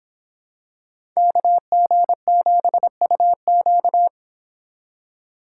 Morse code, heard as a keyed tone, about 27 wpm, 700 Hz. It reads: KG7UQ
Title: Speed Racing - USA 2x2 Call Signs 22wpm